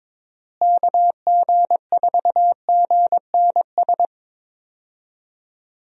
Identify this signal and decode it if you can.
Morse code, heard as a keyed tone, about 22 wpm, 700 Hz. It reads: KG4GNS
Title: USA Call Signs - 6 Characters - Repeated 2x - 22wpm